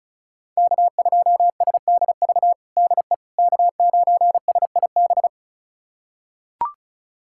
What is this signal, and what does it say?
Morse code, heard as a keyed tone, about 34 wpm, 690 Hz. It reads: K2SDV DE K9SIB
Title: USA Callsign DE Callsign - 5 Characters - 35wpm